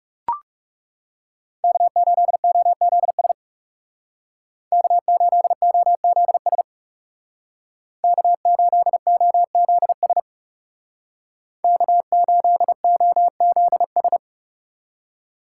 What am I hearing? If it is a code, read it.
Morse code, heard as a keyed tone, about 40 wpm, 700 Hz. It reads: K8OZS K8OZS K8OZS K8OZS